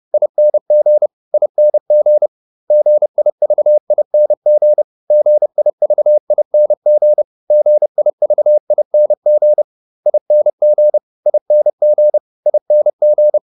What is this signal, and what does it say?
Morse code, heard as a keyed tone, about 30 wpm, 600 Hz. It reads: ING ING GIVING GIVING GIVING ING ING ING